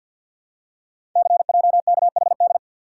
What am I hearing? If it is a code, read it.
Morse code, heard as a keyed tone, about 52 wpm, 700 Hz. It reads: CJKHD
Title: Code Groups - 5 Letter:Numbers - 50wpm